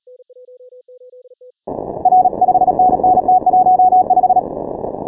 Morse code, heard as a keyed tone, about 37 wpm, 740 Hz. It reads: R 5NN T15
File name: final / W1UJ-168-140-MorseRunner